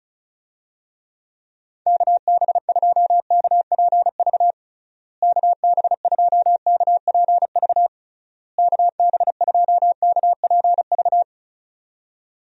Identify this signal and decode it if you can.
Morse code, heard as a keyed tone, about 34 wpm, 700 Hz. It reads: KB2KPV KB2KPV KB2KPV
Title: USA Call Signs - 6 Characters - Repeated 3x - 35wpm